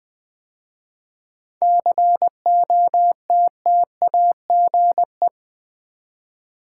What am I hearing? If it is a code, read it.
Morse code, heard as a keyed tone, about 21 wpm, 700 Hz. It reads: COTTAGE
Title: 7 Letter Words - First 3 Letters Given - 20wpm